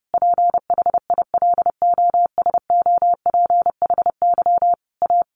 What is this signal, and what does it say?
Morse code, heard as a keyed tone, about 30 wpm, 700 Hz. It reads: PHILOSOPHY A
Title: Sentences from Top 1400 Words 30wpm